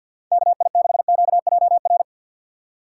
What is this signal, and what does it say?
Morse code, heard as a keyed tone, about 49 wpm, 700 Hz. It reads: KI6QJR
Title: USA Call Signs - 6 Characters - 6x Character Spacing - 50wpm